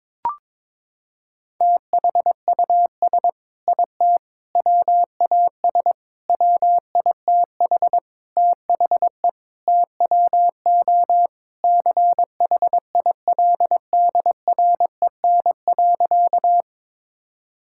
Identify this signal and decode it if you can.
Morse code, heard as a keyed tone, about 22 wpm, 700 Hz. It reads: THUS IT WAS WITH THE TWO CHILDREN.
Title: Sentences from Top 1000 Words - Review Part 5 - 22wpm